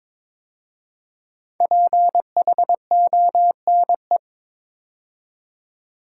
Morse code, heard as a keyed tone, about 22 wpm, 700 Hz. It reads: PHONE
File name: 5 Letter Words - First 3 Letters Given - 22wpm